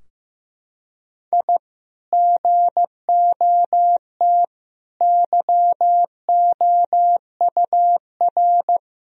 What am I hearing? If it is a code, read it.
Morse code, heard as a keyed tone, about 15 wpm, 700 Hz. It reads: I GOT YOUR
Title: Sentences from Top 1000 Words - Review Part 2 - 15wpm